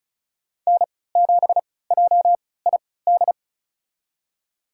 Morse code, heard as a keyed tone, about 34 wpm, spaced as for 20 wpm, 700 Hz. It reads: N7JID
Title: USA Call Signs - 5 Characters - 3x Character Spacing - 35wpm